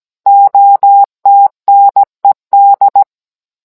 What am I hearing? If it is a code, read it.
Morse code, heard as a keyed tone, about 17 wpm, 800 Hz. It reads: OTNED